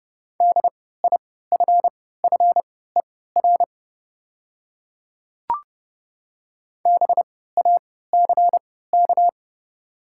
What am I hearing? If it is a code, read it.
Morse code, heard as a keyed tone, about 30 wpm, spaced as for 17 wpm, 700 Hz. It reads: DIFFER BACK
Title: 3x Character Spacing - Top 200 Words - 30wpm